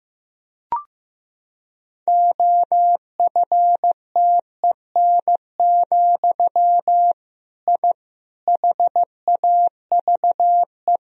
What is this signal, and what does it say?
Morse code, heard as a keyed tone, about 15 wpm, 700 Hz. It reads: OFTEN, I HAVE
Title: Sentences from Top 1000 Words - Review Part 1 - 15wpm